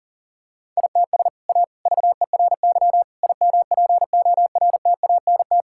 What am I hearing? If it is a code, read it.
Morse code, heard as a keyed tone, about 40 wpm, 700 Hz. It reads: ITS A VERY IMPORTANT
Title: Sentences from Top 1000 Words - Review Part 5 - 40wpm